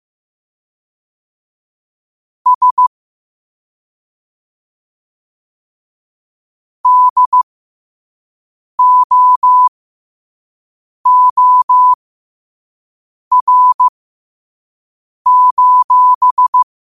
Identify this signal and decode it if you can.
Morse code, heard as a keyed tone, about 15 wpm, spaced as for 5 wpm, 1000 Hz. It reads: S DOOR:
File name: qotdfiles / 5wpm